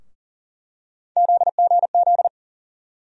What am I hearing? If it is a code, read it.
Morse code, heard as a keyed tone, about 40 wpm, 700 Hz. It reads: ZGZ
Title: USA Call Sign Postfixes - 3 Letters 40wpm